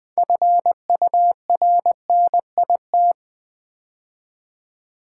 Morse code, heard as a keyed tone, about 20 wpm, 700 Hz. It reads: FURNIT
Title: Build A Word - 3 Letters At A Time - 9-Letter Words - 20wpm